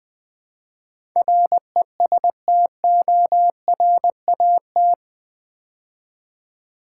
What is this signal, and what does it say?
Morse code, heard as a keyed tone, about 20 wpm, 700 Hz. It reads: RESTORAT
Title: Build A Word - 4 Letters At A Time - 11-Letter Words - 20wpm